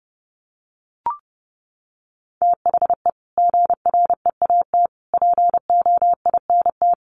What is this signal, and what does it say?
Morse code, heard as a keyed tone, about 28 wpm, 700 Hz. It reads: THE GREAT POINT
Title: Sentences from Top 1500 Words 30wpm